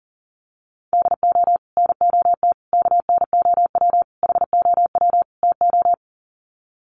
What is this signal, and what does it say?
Morse code, heard as a keyed tone, about 40 wpm, 700 Hz. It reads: DO NOT KNOW HOW TO
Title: Sets of 5 Words 40wpm